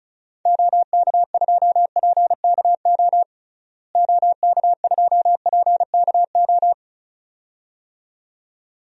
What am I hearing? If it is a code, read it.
Morse code, heard as a keyed tone, about 34 wpm, 700 Hz. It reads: OK2PKO OK2PKO
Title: DXCC-100 Call Signs - 6 Characters - Repeated 2x - 35wpm